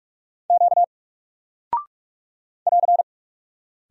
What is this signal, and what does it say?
Morse code, heard as a keyed tone, about 46 wpm, 700 Hz. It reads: Q <AR>
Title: Single Character (Letter:Number:Punctuation:Prosign) Alt 45wpm